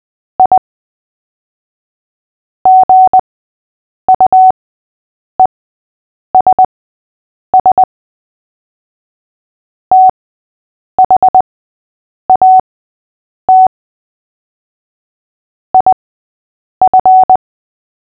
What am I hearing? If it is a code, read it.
Morse code, heard as a keyed tone, about 20 wpm, spaced as for 8 wpm, 750 Hz. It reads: I GUESS THAT IF
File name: cw / fists / Short-Story-8-8wpm